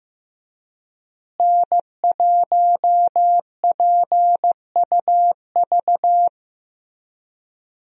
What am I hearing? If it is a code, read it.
Morse code, heard as a keyed tone, about 15 wpm, 700 Hz. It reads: N1PUV
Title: USA Call Signs - 5 Characters - Repeated 2x - 15wpm